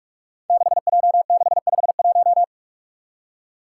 Morse code, heard as a keyed tone, about 44 wpm, 700 Hz. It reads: 6J651
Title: Five Challenging Letter-Number Combinations 45wpm